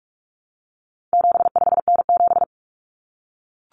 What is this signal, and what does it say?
Morse code, heard as a keyed tone, about 44 wpm, 700 Hz. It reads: 75N7
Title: Common Cut Numbers - 4-Digit 45wpm